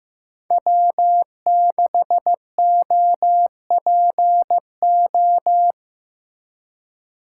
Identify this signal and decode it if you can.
Morse code, heard as a keyed tone, about 15 wpm, 700 Hz. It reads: W6OPO
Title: K1USN SST QSO Practice - 15wpm